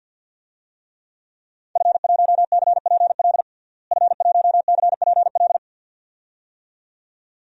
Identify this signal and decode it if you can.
Morse code, heard as a keyed tone, about 47 wpm, 700 Hz. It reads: F1CPL F1CPL
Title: DXCC-100 Call Signs - 5 Characters - Repeated 2x - 50wpm